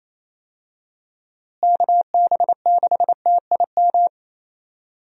Morse code, heard as a keyed tone, about 28 wpm, 700 Hz. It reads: KB6TIM
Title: USA Call Signs - 6 Characters - Repeated 3x - 28wpm